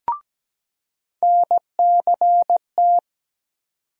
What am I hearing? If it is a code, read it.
Morse code, heard as a keyed tone, about 17 wpm, 700 Hz. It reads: NCT
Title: USA Call Sign Postfixes - 3 Letters 17wpm